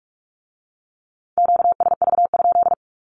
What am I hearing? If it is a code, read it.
Morse code, heard as a keyed tone, about 46 wpm, 700 Hz. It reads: QSV?
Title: Q-Codes with Spoken Meaning - 45wpm